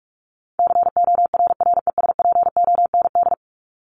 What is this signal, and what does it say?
Morse code, heard as a keyed tone, about 44 wpm, 700 Hz. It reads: CORRESPOND
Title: ING Suffix - Top 100 Words with Base Words 45wpm